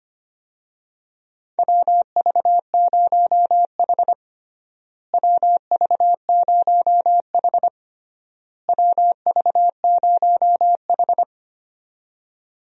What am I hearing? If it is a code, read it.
Morse code, heard as a keyed tone, about 25 wpm, 700 Hz. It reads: WV0H WV0H WV0H
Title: USA Call Signs - 4 Characters - Repeated 3x - 25wpm